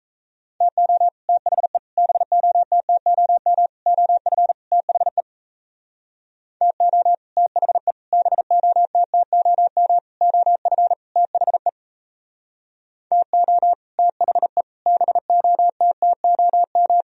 Morse code, heard as a keyed tone, about 36 wpm, 700 Hz. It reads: TO THE BOTTOM OF THE TO THE BOTTOM OF THE TO THE BOTTOM